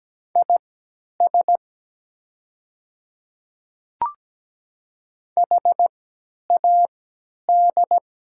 Morse code, heard as a keyed tone, about 17 wpm, spaced as for 10 wpm, 700 Hz. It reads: IS HAD
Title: 3x Character Spacing - Top 100 Words - 17wpm